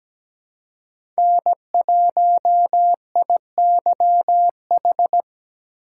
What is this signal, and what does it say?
Morse code, heard as a keyed tone, about 17 wpm, 700 Hz. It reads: N1IYH
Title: USA Call Signs - 5 Characters - 4x Character Spacing - 17wpm